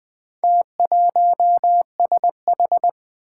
Morse code, heard as a keyed tone, about 20 wpm, 700 Hz. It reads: T1SH
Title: Four Challenging Letter-Number Combinations 20wpm